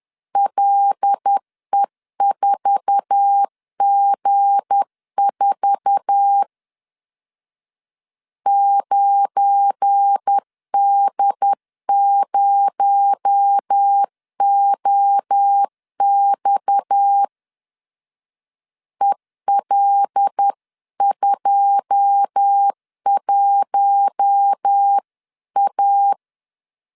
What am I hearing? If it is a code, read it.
Morse code, heard as a keyed tone, about 10 wpm, 800 Hz. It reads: LE4G4 9D0OX EL21A
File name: CW / DET / 1 9 1500C S B 10wpm 4439